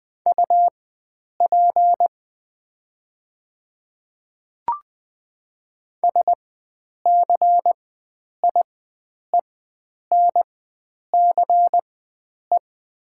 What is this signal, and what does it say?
Morse code, heard as a keyed tone, about 20 wpm, spaced as for 9 wpm, 700 Hz. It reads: UP SCIENCE
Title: 4x Character Spacing - Top 500 Words - 20wpm